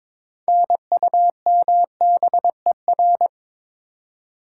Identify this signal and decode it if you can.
Morse code, heard as a keyed tone, about 22 wpm, 700 Hz. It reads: NUMBER
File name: Build A Phrase - 2-Word Phrases - 22wpm